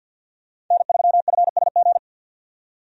Morse code, heard as a keyed tone, about 51 wpm, 700 Hz. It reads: N3FSG
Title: USA Call Signs - 5 Characters - 50wpm